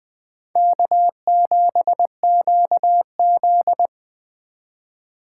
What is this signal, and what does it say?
Morse code, heard as a keyed tone, about 21 wpm, 700 Hz. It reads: K7QZ
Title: Speed Racing - USA 1x2 Call Signs 17wpm